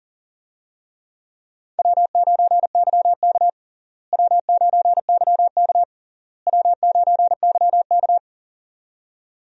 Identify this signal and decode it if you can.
Morse code, heard as a keyed tone, about 40 wpm, 700 Hz. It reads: W9YK W9YK W9YK
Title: USA Call Signs - 4 Characters - Repeated 3x - 40wpm